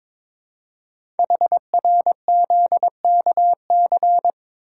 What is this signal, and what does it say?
Morse code, heard as a keyed tone, about 22 wpm, 700 Hz. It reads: HRZKC
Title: Code Groups - 5 Letters - 22wpm